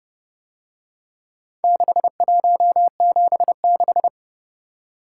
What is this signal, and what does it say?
Morse code, heard as a keyed tone, about 31 wpm, 700 Hz. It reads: 6176